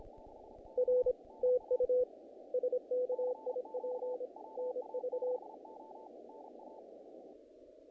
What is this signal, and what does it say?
Morse code, heard as a keyed tone, about 26 wpm, 500 Hz. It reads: R TU SKIP NV
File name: contests / naqp-cw-k5tr-op-k5pi- -1800